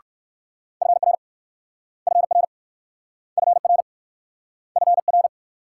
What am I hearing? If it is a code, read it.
Morse code, heard as a keyed tone, about 60 wpm, 690 Hz. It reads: FR FR FR FR